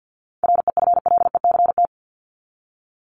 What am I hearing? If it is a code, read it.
Morse code, heard as a keyed tone, about 50 wpm, 700 Hz. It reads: REFLECT
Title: Top 1251-1500 Words 50wpm